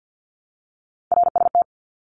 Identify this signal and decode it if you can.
Morse code, heard as a keyed tone, about 48 wpm, 700 Hz. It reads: RST
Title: Abbreviations Repeated 3x - 50wpm